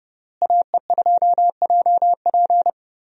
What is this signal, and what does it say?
Morse code, heard as a keyed tone, about 30 wpm, 700 Hz. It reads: AE2JP
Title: USA Call Signs - 5 Characters - Repeated 3x - 30wpm